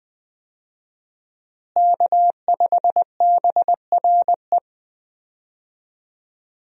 Morse code, heard as a keyed tone, about 20 wpm, 700 Hz. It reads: K5BRE
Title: USA Call Signs - 5 Characters - Repeated 3x - 20wpm